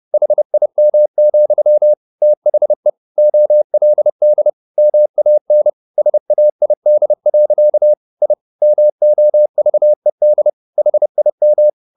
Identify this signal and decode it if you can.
Morse code, heard as a keyed tone, about 30 wpm, 600 Hz. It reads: HIM, THE OLD MAN SAID. I MOVED HIM